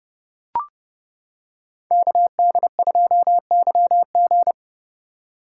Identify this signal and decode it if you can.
Morse code, heard as a keyed tone, about 29 wpm, 700 Hz. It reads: KD2YG